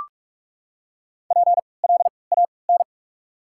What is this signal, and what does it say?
Morse code, heard as a keyed tone, about 44 wpm, spaced as for 25 wpm, 690 Hz. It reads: PLAN
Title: 3x Character Spacing - Top 500 Words - 45wpm